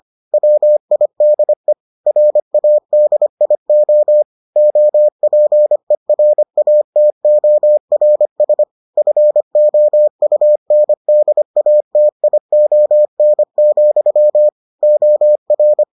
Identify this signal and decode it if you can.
Morse code, heard as a keyed tone, about 25 wpm, 600 Hz. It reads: WIDE RADIO OPERATORS FOUNDATION, OR